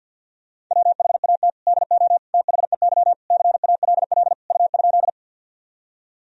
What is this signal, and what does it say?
Morse code, heard as a keyed tone, about 52 wpm, 700 Hz. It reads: WHAT DO THEY CALL U?
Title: Sentences from Top 100 Words and Abbreviations 50wpm